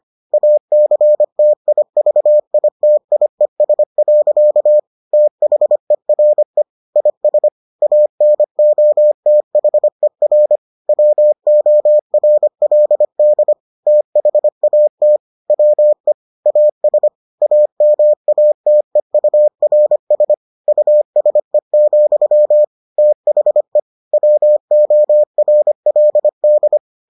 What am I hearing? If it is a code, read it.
Morse code, heard as a keyed tone, about 25 wpm, 600 Hz. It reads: ACTIVITIES. THERE IS ANOTHER WORLD THAT WE AS AMATEURS USE, THE WORLD